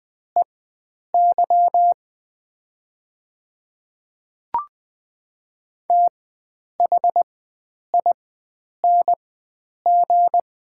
Morse code, heard as a keyed tone, about 20 wpm, spaced as for 9 wpm, 700 Hz. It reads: EY THING
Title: 4x Character Spacing - Top 100 Words - 20wpm